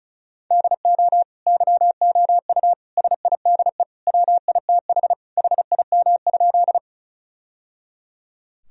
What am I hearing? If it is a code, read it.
Morse code, heard as a keyed tone, about 34 wpm, 700 Hz. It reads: DO YOU SIDE WITH HIM?
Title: Sentences from Top 1000 Words - Review Part 1 - 35wpm